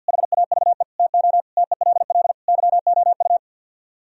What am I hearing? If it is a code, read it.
Morse code, heard as a keyed tone, about 50 wpm, 700 Hz. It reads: HAVE TO TELL YOU
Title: Speed-Racing - Sets of 5 Words 50wpm